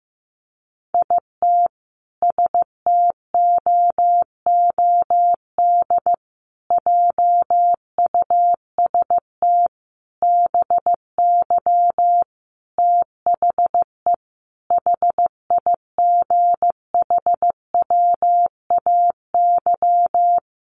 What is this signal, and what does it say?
Morse code, heard as a keyed tone, about 15 wpm, 700 Hz. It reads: IT STOOD JUST BY THE HIGHWAY
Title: Sentences from Top 1400 Words 15wpm